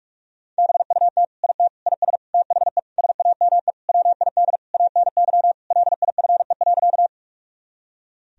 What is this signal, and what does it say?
Morse code, heard as a keyed tone, about 46 wpm, 700 Hz. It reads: BUT IT IS THE SAME WID ANY LIFE.
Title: Sentences from Top 300 Words and Abbreviations 45wpm